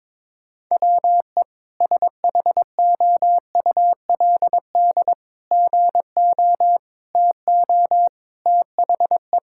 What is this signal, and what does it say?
Morse code, heard as a keyed tone, about 22 wpm, 700 Hz. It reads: WE SHOULD GO TO THE
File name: practice / 22wpm / Sentences with 3-Word Phrases - 22wpm